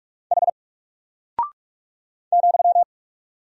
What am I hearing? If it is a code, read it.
Morse code, heard as a keyed tone, about 44 wpm, 700 Hz. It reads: H ,
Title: Single Character (Letter:Number:Punctuation:Prosign) 45wpm